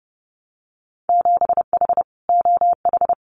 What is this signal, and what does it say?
Morse code, heard as a keyed tone, about 30 wpm, 700 Hz. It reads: 7H OH
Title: Winter Field Day Exchanges Encore 30wpm